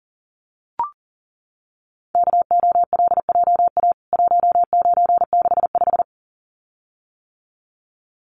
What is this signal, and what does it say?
Morse code, heard as a keyed tone, about 40 wpm, 700 Hz. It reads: KOLJA 1965
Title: CWT Exchange 40wpm